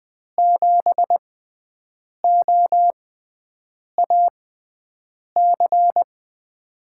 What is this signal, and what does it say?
Morse code, heard as a keyed tone, about 21 wpm, spaced as for 7 wpm, 700 Hz. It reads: 7OAC
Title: USA Call Signs - 6x Character Spacing - 20wpm